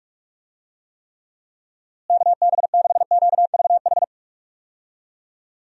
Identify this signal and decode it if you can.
Morse code, heard as a keyed tone, about 41 wpm, 700 Hz. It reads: KB6QVH
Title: USA Call Signs - 6 Characters - Repeated 3x - 45wpm